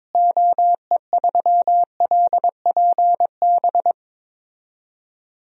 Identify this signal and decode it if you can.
Morse code, heard as a keyed tone, about 21 wpm, 700 Hz. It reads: OE3LPB
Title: DXCC-100 Call Signs - 6 Characters - Repeated 2x - 22wpm